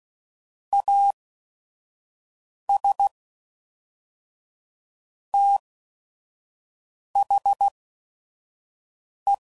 Morse code, heard as a keyed tone, about 16 wpm, spaced as for 5 wpm, 800 Hz. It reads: AS THE
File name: CW Code Groups @5WPM / Technical Subjects / OR Gate